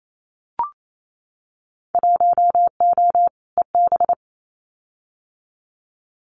Morse code, heard as a keyed tone, about 28 wpm, 700 Hz. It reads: 1O EB
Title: Winter Field Day Exchanges 28wpm